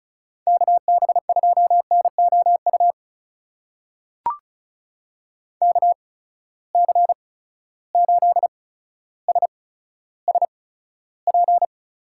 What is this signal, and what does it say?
Morse code, heard as a keyed tone, about 36 wpm, spaced as for 15 wpm, 700 Hz. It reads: KB2NOU KC8SSP
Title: USA Call Signs - 6 Characters - 8x Character Spacing - 35wpm